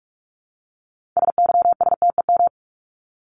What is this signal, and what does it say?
Morse code, heard as a keyed tone, about 44 wpm, 700 Hz. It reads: SYSTEM
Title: ES:S Suffix - Top 100 Words with Base Words 45wpm